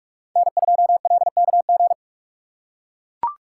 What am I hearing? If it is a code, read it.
Morse code, heard as a keyed tone, about 44 wpm, 700 Hz. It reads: N2LKG